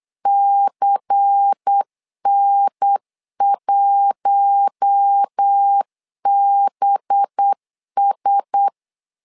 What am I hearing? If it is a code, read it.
Morse code, heard as a keyed tone, about 8 wpm, 800 Hz. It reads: CN1BS
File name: LEZIONI / 5 LBVC / 5 LBVC 8wpm 2935